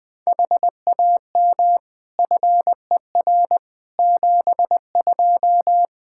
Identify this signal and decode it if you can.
Morse code, heard as a keyed tone, about 20 wpm, 700 Hz. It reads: HAM FER 72
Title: K1USN SST QSO Practice 20wpm